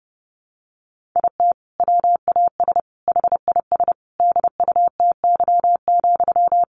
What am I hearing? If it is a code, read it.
Morse code, heard as a keyed tone, about 30 wpm, 700 Hz. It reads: IT WAS HIS DUTY,
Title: Sentences from Top 1500 Words 30wpm